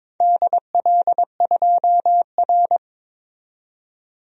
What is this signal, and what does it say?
Morse code, heard as a keyed tone, about 22 wpm, 700 Hz. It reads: DL2R
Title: DXCC-100 Call Signs - 4 Characters - 22wpm